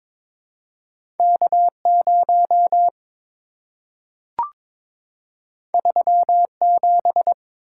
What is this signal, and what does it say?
Morse code, heard as a keyed tone, about 22 wpm, 700 Hz. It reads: K0 37
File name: practice / 22wpm / Two Letter-Number Combinations - 22wpm